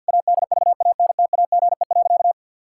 Morse code, heard as a keyed tone, about 52 wpm, 700 Hz. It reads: ADVANTAGE.
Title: Sentences from Top 1000 Words - Review Part 6 - 50wpm